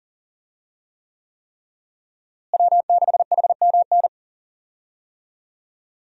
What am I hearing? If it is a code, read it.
Morse code, heard as a keyed tone, about 40 wpm, 700 Hz. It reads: W6HMN
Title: USA Call Signs - 5 Characters - Repeated 3x - 40wpm